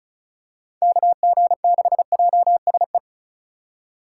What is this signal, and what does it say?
Morse code, heard as a keyed tone, about 36 wpm, 700 Hz. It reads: KG6JSE
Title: USA Call Signs - 6 Characters - 8x Character Spacing - 35wpm